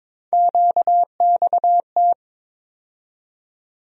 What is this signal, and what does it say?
Morse code, heard as a keyed tone, about 22 wpm, 700 Hz. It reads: QXT
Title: USA Call Sign Postfixes - 3 Letters 22wpm